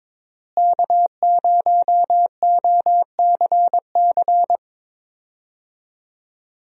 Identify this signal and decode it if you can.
Morse code, heard as a keyed tone, about 22 wpm, 700 Hz. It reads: K0OCC
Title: USA Call Signs - 5 Characters - Repeated 2x - 22wpm